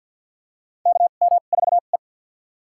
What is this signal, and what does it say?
Morse code, heard as a keyed tone, about 53 wpm, spaced as for 36 wpm, 700 Hz. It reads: KM4E
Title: USA Call Signs - 4 Characters - 2x Character Spacing - 50wpm